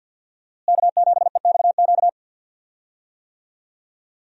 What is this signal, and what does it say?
Morse code, heard as a keyed tone, about 50 wpm, 700 Hz. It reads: K7EXQ
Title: USA Call Signs - 5 Characters - Repeated 3x - 50wpm